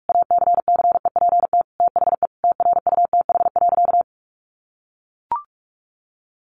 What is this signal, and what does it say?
Morse code, heard as a keyed tone, about 44 wpm, 700 Hz. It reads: ACCEPT THE TRUTH.
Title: Sentences from Top 1300 Words 45wpm